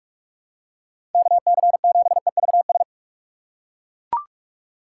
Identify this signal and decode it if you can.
Morse code, heard as a keyed tone, about 41 wpm, 700 Hz. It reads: KC7EVS
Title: USA Call Signs - 6 Characters - 3x Character Spacing - 45wpm